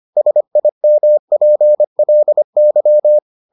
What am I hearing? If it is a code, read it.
Morse code, heard as a keyed tone, about 25 wpm, 600 Hz. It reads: SIMPLY